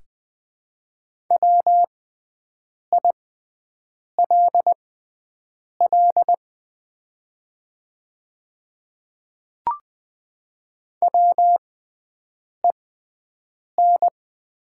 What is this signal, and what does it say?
Morse code, heard as a keyed tone, about 20 wpm, spaced as for 7 wpm, 700 Hz. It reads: WILL WEN
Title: 6x Character Spacing - Top 500 Words - 20wpm